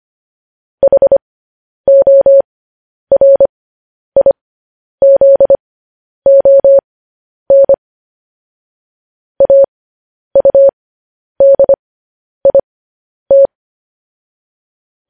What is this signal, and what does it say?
Morse code, heard as a keyed tone, about 25 wpm, spaced as for 10 wpm, 550 Hz. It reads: HORIZON AUDIT
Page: news-headlines-in-morse-code-at-10-wpm